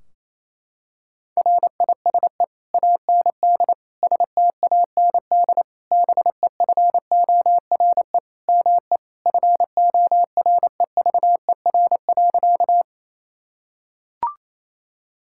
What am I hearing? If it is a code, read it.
Morse code, heard as a keyed tone, about 28 wpm, 700 Hz. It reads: RISE AND STAND BEFORE ME FOREVER.
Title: Sentences from Top 1600 Words 28wpm